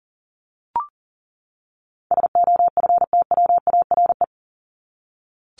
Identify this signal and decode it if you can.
Morse code, heard as a keyed tone, about 40 wpm, 700 Hz. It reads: SOFTWARE
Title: Top 501-750 Words 40wpm